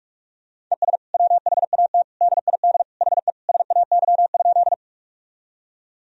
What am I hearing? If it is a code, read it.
Morse code, heard as a keyed tone, about 43 wpm, 700 Hz. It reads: ES WHAT DID HE SAY?
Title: Sentences from Top 200 Words and Abbreviations 45wpm